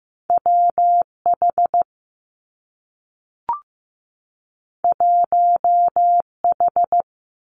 Morse code, heard as a keyed tone, about 15 wpm, 700 Hz. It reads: W H 1 H
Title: Two Challenging Letter-Number Combinations 15wpm